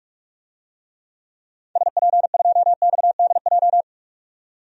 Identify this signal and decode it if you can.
Morse code, heard as a keyed tone, about 47 wpm, 700 Hz. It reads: SP2XDJ
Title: DXCC-100 Call Signs - 6 Characters - 45wpm